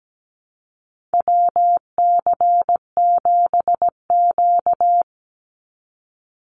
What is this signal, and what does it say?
Morse code, heard as a keyed tone, about 17 wpm, 700 Hz. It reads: WC7Q
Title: Call Signs - 17wpm